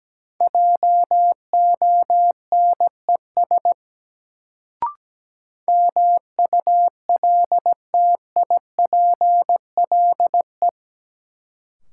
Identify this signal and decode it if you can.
Morse code, heard as a keyed tone, about 17 wpm, 700 Hz. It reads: JONES MULTIPLE
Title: Sets of 5 Words - New Word Drill - 17wpm